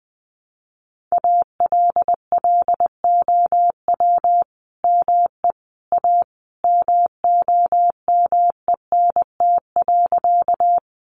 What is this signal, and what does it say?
Morse code, heard as a keyed tone, about 19 wpm, 700 Hz. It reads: ALLOW ME A MOMENT.
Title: Sentences from Top 1000 Words 20wpm